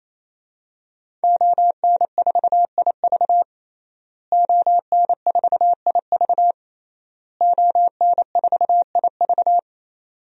Morse code, heard as a keyed tone, about 28 wpm, 700 Hz. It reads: ON4IV ON4IV ON4IV
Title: DXCC-100 Call Signs - 5 Characters - Repeated 3x - 28wpm